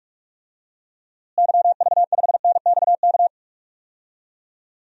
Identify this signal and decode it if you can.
Morse code, heard as a keyed tone, about 46 wpm, 700 Hz. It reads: YV5NXK
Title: DXCC-100 Call Signs - 6 Characters - Repeated 2x - 45wpm